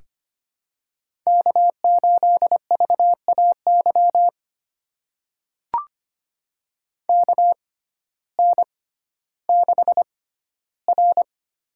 Morse code, heard as a keyed tone, about 25 wpm, spaced as for 14 wpm, 700 Hz. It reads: K8VAY KN6R
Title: USA Call Signs - 6x Character Spacing - 25wpm